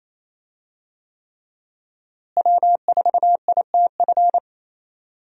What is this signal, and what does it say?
Morse code, heard as a keyed tone, about 27 wpm, 700 Hz. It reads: W4ITF
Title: USA Call Signs - 5 Characters - Repeated 3x - 28wpm